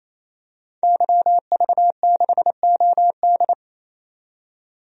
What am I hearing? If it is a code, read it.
Morse code, heard as a keyed tone, about 28 wpm, 700 Hz. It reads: YV6OD
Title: DXCC-100 Call Signs - 5 Characters - 28wpm